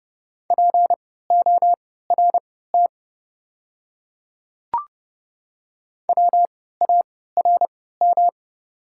Morse code, heard as a keyed tone, about 31 wpm, spaced as for 17 wpm, 690 Hz. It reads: PORT WARM
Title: 3x Character Spacing - Top 500 Words - 30wpm